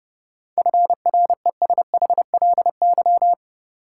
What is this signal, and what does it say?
Morse code, heard as a keyed tone, about 30 wpm, 700 Hz. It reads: FRESHLY
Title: Sets of 5 Words Encore - New Word Drill - 30wpm